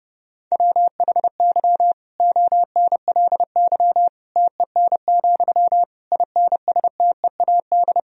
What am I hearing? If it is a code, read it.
Morse code, heard as a keyed tone, about 30 wpm, 700 Hz. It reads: WHY ONLY TEN, INSTEAD
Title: Sentences from Top 1000 Words - Review Part 4 - 30wpm